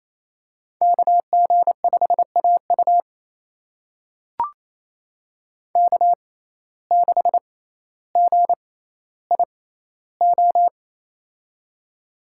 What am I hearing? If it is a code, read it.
Morse code, heard as a keyed tone, about 28 wpm, spaced as for 14 wpm, 700 Hz. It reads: KG5AU K6GIO
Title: USA Call Signs - 6x Character Spacing - 28wpm